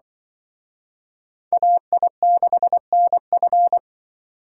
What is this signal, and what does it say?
Morse code, heard as a keyed tone, about 24 wpm, 700 Hz. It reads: AI6NF
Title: Speed Racing - USA 2x2 Call Signs 20wpm